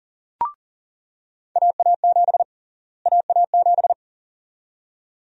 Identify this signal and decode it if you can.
Morse code, heard as a keyed tone, about 40 wpm, 700 Hz. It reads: AA7 AA7
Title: USA Call Sign Prefixes - Repeated 2x - 40wpm